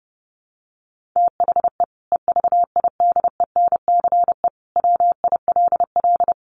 Morse code, heard as a keyed tone, about 30 wpm, 700 Hz. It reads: THE EVIDENCE WILL
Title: Sentences from Top 1300 Words 30wpm